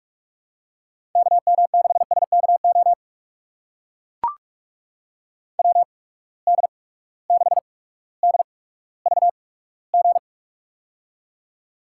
Morse code, heard as a keyed tone, about 44 wpm, spaced as for 19 wpm, 700 Hz. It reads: KM6SKO WD6DVG
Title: USA Call Signs - 6 Characters - 8x Character Spacing - 45wpm